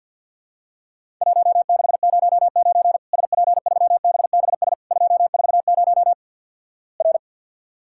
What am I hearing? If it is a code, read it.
Morse code, heard as a keyed tone, about 48 wpm, 700 Hz. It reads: 1609 SP3BBS 240 R
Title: Worked All Europe DX Contest - QTC Exchange with Repeats - 50wpm